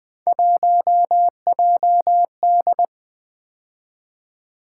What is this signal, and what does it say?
Morse code, heard as a keyed tone, about 20 wpm, 700 Hz. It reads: 1JD
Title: Three Letter-Number Combinations - 20wpm